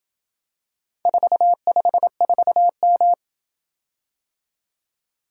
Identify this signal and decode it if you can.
Morse code, heard as a keyed tone, about 27 wpm, 700 Hz. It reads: 454M